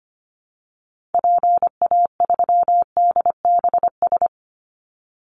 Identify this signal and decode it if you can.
Morse code, heard as a keyed tone, about 25 wpm, 700 Hz. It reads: PA3DBS
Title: Call Signs Repeated 3x - 25wpm